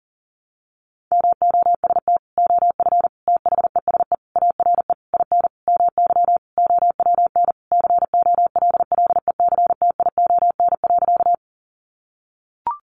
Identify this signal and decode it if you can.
Morse code, heard as a keyed tone, about 40 wpm, 700 Hz. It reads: MOST OF THESE ARE IN MY OWN COLLECTION.
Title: Sentences from Top 1200 Words 40wpm